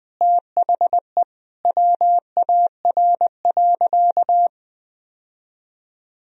Morse code, heard as a keyed tone, about 20 wpm, 700 Hz. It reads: THE WAR.
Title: Sentences from Top 1000 Words - Review Part 4 - 20wpm